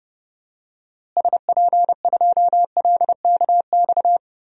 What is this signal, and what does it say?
Morse code, heard as a keyed tone, about 30 wpm, 700 Hz. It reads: SP2LKX
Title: DXCC-100 Call Signs - 6 Characters - Repeated 3x - 30wpm